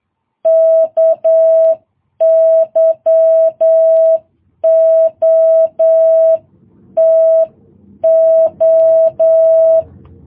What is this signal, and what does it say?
Morse code, heard as a keyed tone, about 8 wpm, 650 Hz. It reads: KYOTO
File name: jh8cbh